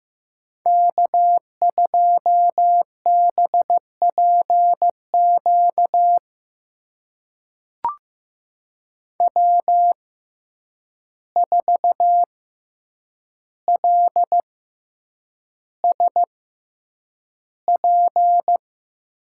Morse code, heard as a keyed tone, about 15 wpm, spaced as for 8 wpm, 700 Hz. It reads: K2BPQ W4LSP
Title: USA Call Signs - 5 Characters - 6x Character Spacing - 15wpm